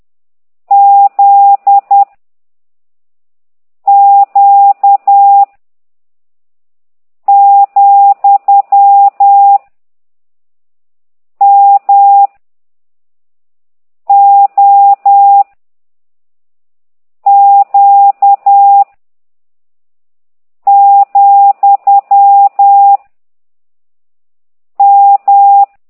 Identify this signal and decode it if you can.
Morse code, heard as a keyed tone, about 10 wpm, 800 Hz. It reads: Z Q , M O Q , M